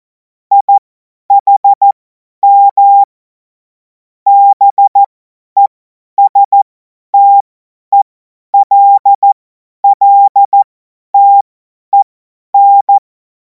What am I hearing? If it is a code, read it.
Morse code, heard as a keyed tone, about 14 wpm, spaced as for 10 wpm, 800 Hz. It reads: IHM BESTELLTEN
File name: Morsekurs / Witze / Witz 28. April 2026 50 F70